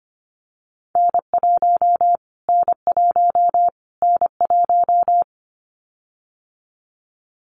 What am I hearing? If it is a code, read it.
Morse code, heard as a keyed tone, about 25 wpm, 700 Hz. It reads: N1 N1 N1
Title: Callsign Prefixes for Top 25 DXCC - Repeated 3x - 25wpm